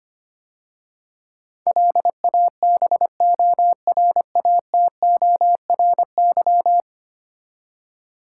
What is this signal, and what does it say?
Morse code, heard as a keyed tone, about 25 wpm, 700 Hz. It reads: LABORATORY
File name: 10 Letter Words - First 3 Letters Given - 25wpm